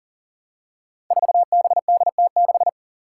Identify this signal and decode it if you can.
Morse code, heard as a keyed tone, about 40 wpm, 700 Hz. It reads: 4BDT6
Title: Five Challenging Letter-Number Combinations 40wpm